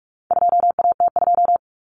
Morse code, heard as a keyed tone, about 44 wpm, 700 Hz. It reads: 2AT2
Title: Common Cut Numbers - 4-Digit 45wpm